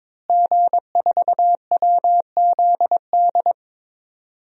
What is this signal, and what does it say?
Morse code, heard as a keyed tone, about 21 wpm, 700 Hz. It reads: G4WZD
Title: DXCC-100 Call Signs - 5 Characters - Repeated 2x - 22wpm